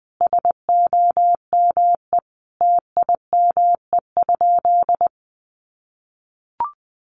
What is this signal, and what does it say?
Morse code, heard as a keyed tone, about 20 wpm, 700 Hz. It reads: SOME TIME?
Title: Sentences from Top 600 Words 20wpm